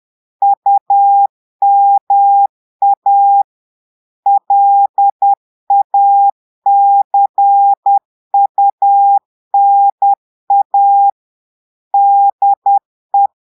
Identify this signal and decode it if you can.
Morse code, heard as a keyed tone, about 10 wpm, 800 Hz. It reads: UMA LACUNA DE
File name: Portugais - texte en clair 1 - vitesse - 10 mots minute